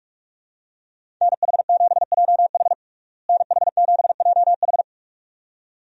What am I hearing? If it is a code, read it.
Morse code, heard as a keyed tone, about 46 wpm, 700 Hz. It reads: NH7JH NH7JH
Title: USA Call Signs - 5 Characters - Repeated 2x - 45wpm